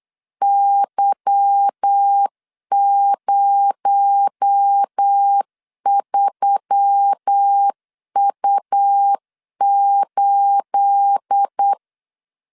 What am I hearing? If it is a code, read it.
Morse code, heard as a keyed tone, about 8 wpm, 800 Hz. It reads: Y03U8
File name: LEZIONI / 9 28 8wpm 3512